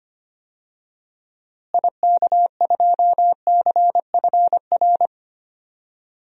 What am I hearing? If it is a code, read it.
Morse code, heard as a keyed tone, about 25 wpm, 700 Hz. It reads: IK2CFR
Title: DXCC-100 Call Signs - 6 Characters - Repeated 2x - 25wpm